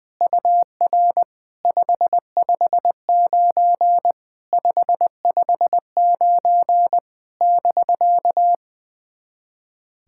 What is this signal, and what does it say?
Morse code, heard as a keyed tone, about 20 wpm, 700 Hz. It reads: UR 559 559 <BK>
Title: SOTA QSO Practice 20wpm